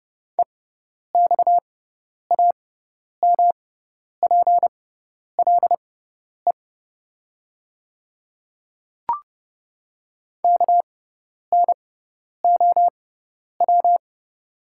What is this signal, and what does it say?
Morse code, heard as a keyed tone, about 30 wpm, spaced as for 10 wpm, 700 Hz. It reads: EXAMPLE KNOW
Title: 6x Character Spacing - Top 300 Words - 30wpm